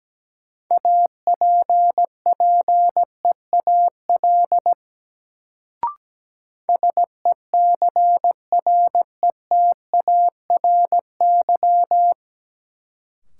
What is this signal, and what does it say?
Morse code, heard as a keyed tone, about 17 wpm, 700 Hz. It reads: APPEAL SECRETARY
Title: Sentences from Top 1300 Words - New Word Drill Alt - 17wpm